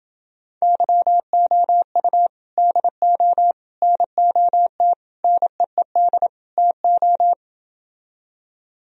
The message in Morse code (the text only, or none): YOU DO NOT NEED TO